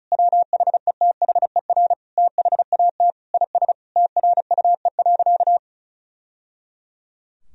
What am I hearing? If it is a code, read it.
Morse code, heard as a keyed tone, about 34 wpm, 700 Hz. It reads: WHETHER THAT IS TRUE.
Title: Sentences from Top 1000 Words - Review Part 4 - 35wpm